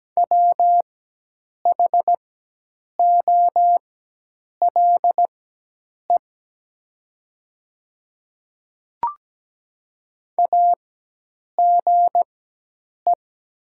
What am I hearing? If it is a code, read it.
Morse code, heard as a keyed tone, about 17 wpm, spaced as for 8 wpm, 700 Hz. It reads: WHOLE AGE
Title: 4x Character Spacing - Top 500 Words - 17wpm